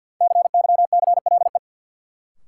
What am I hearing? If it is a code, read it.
Morse code, heard as a keyed tone, about 53 wpm, 700 Hz. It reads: CYCLE
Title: Sentences from Top 900 Words - New Word Drill - 50wpm